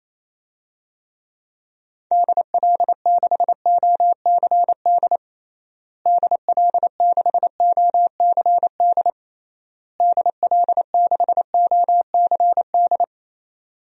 Morse code, heard as a keyed tone, about 28 wpm, 700 Hz. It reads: DL6OCD DL6OCD DL6OCD
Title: DXCC-100 Call Signs - 6 Characters - Repeated 3x - 28wpm